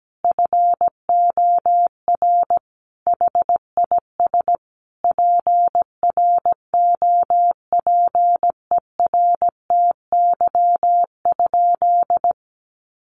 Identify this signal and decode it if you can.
Morse code, heard as a keyed tone, about 17 wpm, 700 Hz. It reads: FOR HIS PROPERTY?
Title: Sentences from Top 1500 Words 17wpm